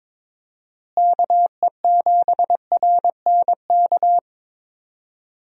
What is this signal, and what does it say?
Morse code, heard as a keyed tone, about 22 wpm, 700 Hz. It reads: KE7RNK
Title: USA Call Signs - 6 Characters - Repeated 2x - 22wpm